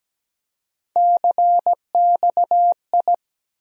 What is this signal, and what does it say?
Morse code, heard as a keyed tone, about 17 wpm, 700 Hz. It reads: CXI
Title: USA Call Sign Postfixes - 3 Letters 17wpm